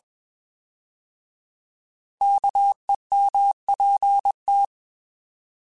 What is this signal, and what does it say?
Morse code, heard as a keyed tone, about 21 wpm, 800 Hz. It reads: KEMPT
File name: Alphabetical / Alphabetical mp3s / Ks @20